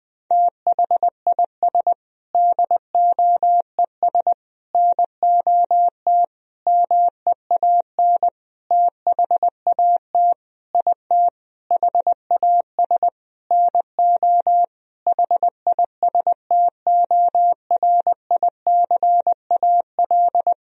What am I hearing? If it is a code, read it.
Morse code, heard as a keyed tone, about 20 wpm, 700 Hz. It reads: THIS DOES NOT MEAN THAT IT HAS NO HISTORICAL
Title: Sentences from Top 1000 Words - Review Part 4 - 20wpm